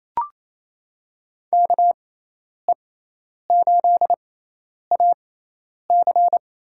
Morse code, heard as a keyed tone, about 28 wpm, spaced as for 10 wpm, 700 Hz. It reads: KE8AC
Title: USA Call Signs - 6x Character Spacing - 28wpm